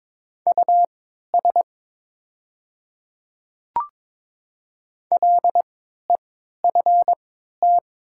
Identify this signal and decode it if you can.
Morse code, heard as a keyed tone, about 22 wpm, spaced as for 13 wpm, 700 Hz. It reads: US LEFT
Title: 3x Character Spacing - Top 500 Words - 22wpm